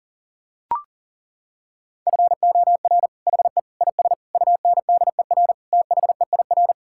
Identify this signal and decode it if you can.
Morse code, heard as a keyed tone, about 40 wpm, 700 Hz. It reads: FOR HE IS UNDER THEIR
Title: Sentences from Top 1600 Words 40wpm